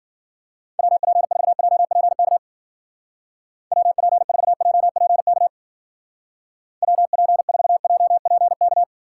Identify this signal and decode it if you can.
Morse code, heard as a keyed tone, about 52 wpm, 700 Hz. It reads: WP4JPK WP4JPK WP4JPK